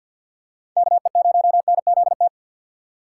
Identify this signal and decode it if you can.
Morse code, heard as a keyed tone, about 52 wpm, 700 Hz. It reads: KE0NZT
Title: USA Call Signs - 6 Characters - 3x Character Spacing - 50wpm